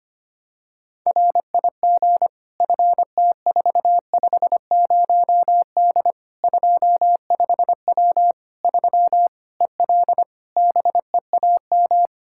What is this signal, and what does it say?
Morse code, heard as a keyed tone, about 25 wpm, 700 Hz. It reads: RIG FT450D 25W 3 EL BEAM
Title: Practice QSO - Exchange 2 - Info - 25wpm